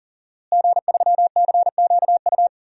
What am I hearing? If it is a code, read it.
Morse code, heard as a keyed tone, about 38 wpm, 700 Hz. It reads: G3CQU
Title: DXCC-100 Call Signs - 5 Characters - Repeated 2x - 40wpm